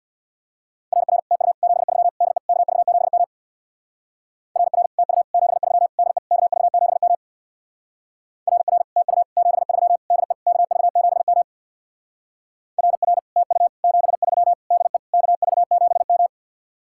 Morse code, heard as a keyed tone, about 64 wpm, 700 Hz. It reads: RR TU 73 DE KV7M RR TU 73 DE KV7M RR TU 73 DE KV7M RR TU 73 DE KV7M